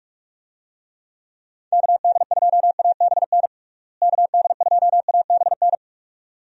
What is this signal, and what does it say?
Morse code, heard as a keyed tone, about 46 wpm, 700 Hz. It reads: KD2ABN KD2ABN
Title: USA Call Signs - 6 Characters - Repeated 2x - 45wpm